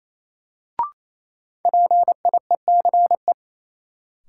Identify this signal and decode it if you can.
Morse code, heard as a keyed tone, about 28 wpm, 700 Hz. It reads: PIECE
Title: Sentences from Top 400 Words - New Word Drill - 28wpm